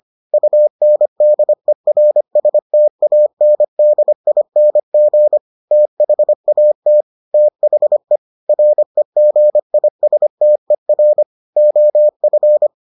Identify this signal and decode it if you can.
Morse code, heard as a keyed tone, about 25 wpm, 600 Hz. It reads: UNDERSTANDING THAT THE REGISTER OF